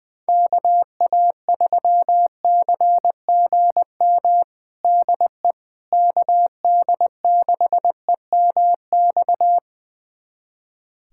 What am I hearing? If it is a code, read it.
Morse code, heard as a keyed tone, about 21 wpm, 700 Hz. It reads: KA3CGM DE KD6EMX
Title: USA Callsign DE Callsign 20wpm